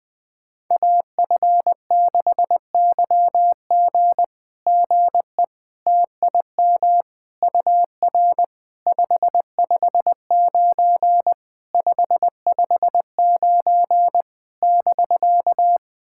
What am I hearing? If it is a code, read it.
Morse code, heard as a keyed tone, about 20 wpm, 700 Hz. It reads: AF6YG GE TIM UR 559 559 <BK>
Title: SOTA QSO Practice 20wpm